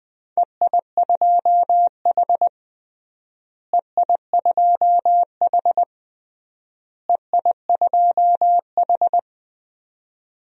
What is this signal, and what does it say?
Morse code, heard as a keyed tone, about 20 wpm, 700 Hz. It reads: EI2H EI2H EI2H
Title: DXCC-100 Call Signs - 4 Characters - Repeated 3x - 20wpm